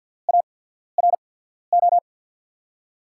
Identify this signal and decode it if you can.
Morse code, heard as a keyed tone, about 53 wpm, spaced as for 14 wpm, 700 Hz. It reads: ARO
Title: USA Call Signs - 6 Characters - 8x Character Spacing - 50wpm